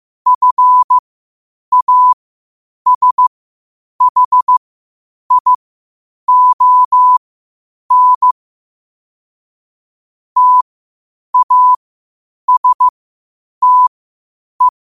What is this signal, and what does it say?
Morse code, heard as a keyed tone, about 15 wpm, spaced as for 8 wpm, 1000 Hz. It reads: FASHION TASTE
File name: 8wpm